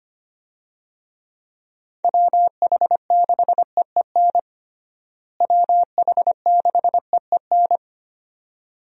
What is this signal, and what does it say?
Morse code, heard as a keyed tone, about 25 wpm, 700 Hz. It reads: WH6EEN WH6EEN
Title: USA Call Signs - 6 Characters - Repeated 2x - 25wpm